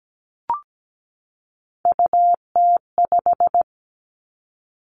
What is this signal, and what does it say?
Morse code, heard as a keyed tone, about 17 wpm, 700 Hz. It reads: UT5
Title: Callsign Prefixes for Top 25 DXCC - 17wpm